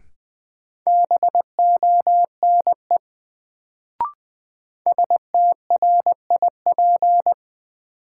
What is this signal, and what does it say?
Morse code, heard as a keyed tone, about 20 wpm, 700 Hz. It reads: BONE STRIP
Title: Sentences from Top 1600 Words - New Word Drill Alt - 20wpm